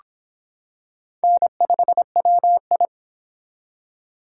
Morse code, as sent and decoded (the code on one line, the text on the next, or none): -. ..... .-- ..
N5WI